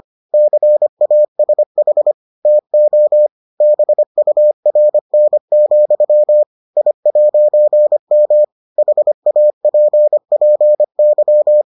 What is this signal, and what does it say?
Morse code, heard as a keyed tone, about 25 wpm, 600 Hz. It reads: CASH TO BURN, I'M HAPPY